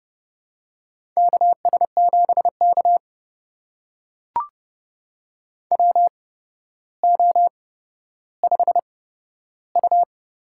Morse code, heard as a keyed tone, about 30 wpm, spaced as for 13 wpm, 700 Hz. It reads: KS7K WO5U
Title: USA Call Signs - 4 Characters - 8x Character Spacing - 30wpm